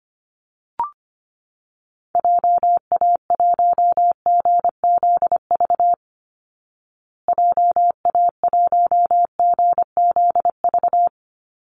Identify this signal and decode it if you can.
Morse code, heard as a keyed tone, about 25 wpm, 700 Hz. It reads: JA1GZV JA1GZV
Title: Call Signs Repeated 2x - 25wpm